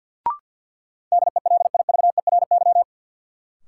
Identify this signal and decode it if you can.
Morse code, heard as a keyed tone, about 52 wpm, 700 Hz. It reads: DELIVERY
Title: Sentences from Top 1500 Words - New Word Drill - 50wpm